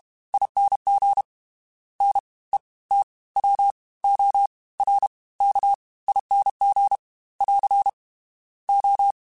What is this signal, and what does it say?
Morse code, heard as a keyed tone, about 32 wpm, 800 Hz. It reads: ING N E T W O R K ING <AR> O
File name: Special INGs / Special INGs-mp3s / Fast ing @30